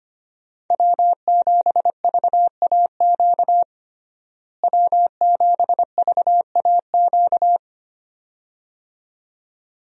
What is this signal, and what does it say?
Morse code, heard as a keyed tone, about 25 wpm, 700 Hz. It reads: W7VAQ W7VAQ
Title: USA Call Signs - 5 Characters - Repeated 2x - 25wpm